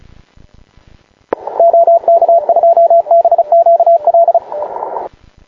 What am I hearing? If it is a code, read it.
Morse code, heard as a keyed tone, about 34 wpm, 680 Hz, with radio noise. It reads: OK2BQL
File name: klub / sat 0130 w1zt n2nt